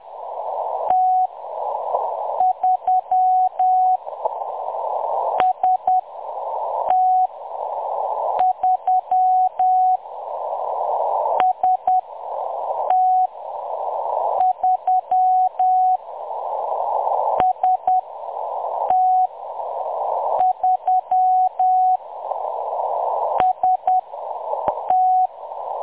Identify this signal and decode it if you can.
Morse code, heard as a keyed tone, about 10 wpm, 740 Hz. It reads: T 3 S T 3 S T 3 S T 3 S T
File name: M23-ST3- -0858utc-10184khz